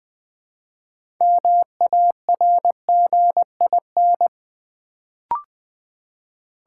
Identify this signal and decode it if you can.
Morse code, heard as a keyed tone, about 20 wpm, 700 Hz. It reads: MARGIN
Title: Build A Word - 3 Letters At A Time - 6-Letter Words - 20wpm